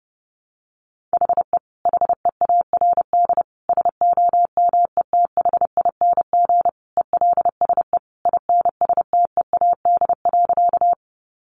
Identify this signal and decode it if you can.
Morse code, heard as a keyed tone, about 30 wpm, 700 Hz. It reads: HE HEARD SOMETHING ELSE INSTEAD.
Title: Sentences from Top 800 Words 30wpm